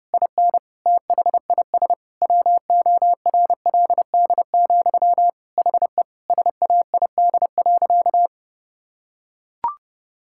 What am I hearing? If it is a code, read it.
Morse code, heard as a keyed tone, about 30 wpm, 700 Hz. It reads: IN THIS WORLD, HE SAID.
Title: Sentences from Top 1600 Words 30wpm